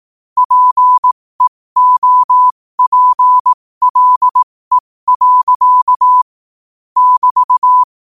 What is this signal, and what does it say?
Morse code, heard as a keyed tone, about 18 wpm, spaced as for 15 wpm, 1000 Hz. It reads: PEOPLE. <BT>
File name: qotdfiles / 15wpm